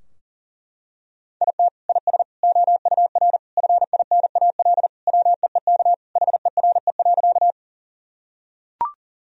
Morse code, heard as a keyed tone, about 40 wpm, 700 Hz. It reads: IT IS OUR FINAL WEEK HERE.
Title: Sentences from Top 1000 Words - Review Part 2 - 40wpm